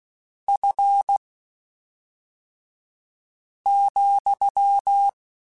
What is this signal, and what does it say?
Morse code, heard as a keyed tone, about 16 wpm, spaced as for 3 wpm, 800 Hz. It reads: F,
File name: Spaced Characters / Spaced Characters-mp3s / Weather 3 17 sp @5